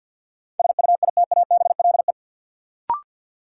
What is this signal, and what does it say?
Morse code, heard as a keyed tone, about 50 wpm, 700 Hz. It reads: SUITABLE
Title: Sentences from Top 1400 Words - New Word Drill Alt - 50wpm